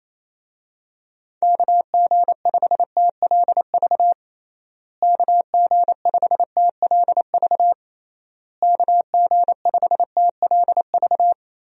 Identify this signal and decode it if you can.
Morse code, heard as a keyed tone, about 28 wpm, 700 Hz. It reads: KG5TLV KG5TLV KG5TLV
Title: USA Call Signs - 6 Characters - Repeated 3x - 28wpm